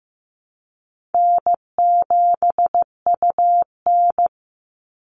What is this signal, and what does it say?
Morse code, heard as a keyed tone, about 15 wpm, 700 Hz. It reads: N7UN
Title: NA SOTA Activator Callsigns 15wpm